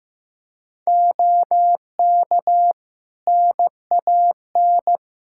Morse code, heard as a keyed tone, about 15 wpm, 700 Hz. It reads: OK NAN
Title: Practice QSO - Exchange 3 - More Info - 15wpm